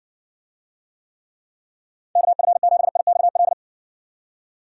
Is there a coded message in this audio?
Yes: Morse code from 2.2 to 3.5 s, heard as a keyed tone, about 60 wpm, 700 Hz. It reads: KF7IXL